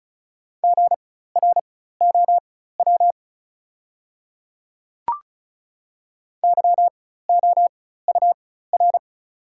Morse code, heard as a keyed tone, about 36 wpm, spaced as for 16 wpm, 700 Hz. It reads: GROW YOUR
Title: 4x Character Spacing - Top 500 Words - 35wpm